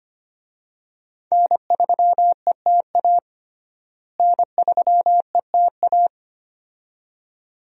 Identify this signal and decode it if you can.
Morse code, heard as a keyed tone, about 25 wpm, 700 Hz. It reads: N3ETA N3ETA
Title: USA Call Signs - 5 Characters - Repeated 2x - 25wpm